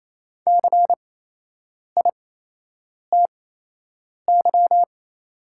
Morse code, heard as a keyed tone, about 28 wpm, spaced as for 8 wpm, 690 Hz. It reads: CITY
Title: 8x Character Spacing - Top 500 Words - 28wpm